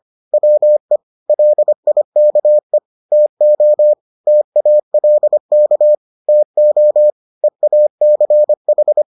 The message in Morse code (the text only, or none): WE LIKE TO TALK TO EACH